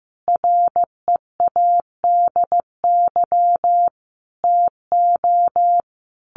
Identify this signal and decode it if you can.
Morse code, heard as a keyed tone, about 15 wpm, 700 Hz. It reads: READY TO
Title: Sentences from Top 1400 Words 15wpm